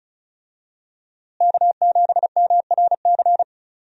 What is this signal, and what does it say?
Morse code, heard as a keyed tone, about 34 wpm, 700 Hz. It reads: K7MRC
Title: USA Call Signs - 5 Characters - Repeated 2x - 35wpm